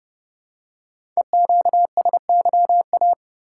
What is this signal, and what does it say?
Morse code, heard as a keyed tone, about 30 wpm, 700 Hz. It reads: EQSYA
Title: Code Groups - 5 Letters - 30wpm